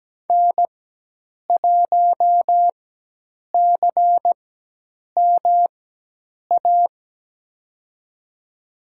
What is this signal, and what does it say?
Morse code, heard as a keyed tone, about 17 wpm, spaced as for 8 wpm, 700 Hz. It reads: N1CMA
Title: USA Call Signs - 5 Characters - 4x Character Spacing - 17wpm